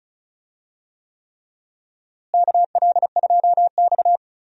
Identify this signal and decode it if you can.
Morse code, heard as a keyed tone, about 36 wpm, 700 Hz. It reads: KL2X
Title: USA Call Signs - 4 Characters - Repeated 3x - 35wpm